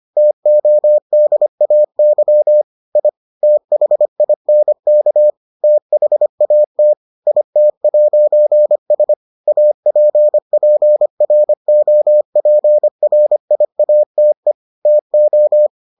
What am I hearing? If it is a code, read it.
Morse code, heard as a keyed tone, about 25 wpm, 600 Hz. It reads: TODAY I THINK THAT IT'S APPROPRIATE TO